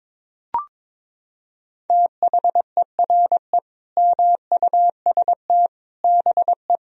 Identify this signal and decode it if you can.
Morse code, heard as a keyed tone, about 22 wpm, 700 Hz. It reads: THERE MUST BE
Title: Sentences from Top 1000 Words - Review Part 3 - 22wpm